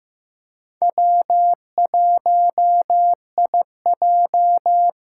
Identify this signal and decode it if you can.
Morse code, heard as a keyed tone, about 15 wpm, 700 Hz. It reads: W1IJ
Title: USA Call Signs - 4 Characters - 4x Character Spacing - 15wpm